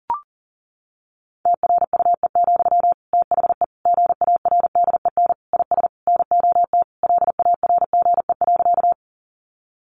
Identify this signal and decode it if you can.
Morse code, heard as a keyed tone, about 40 wpm, 700 Hz. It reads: TRUE, THE GARDEN IS NOT LARGE.
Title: Sentences from Top 700 Words 40wpm